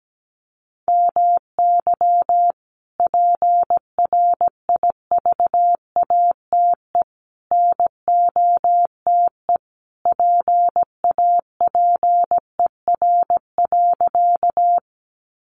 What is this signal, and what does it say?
Morse code, heard as a keyed tone, about 17 wpm, 700 Hz. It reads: MY PRIVATE NOTE PAPER.
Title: Sentences from Top 1000 Words 17wpm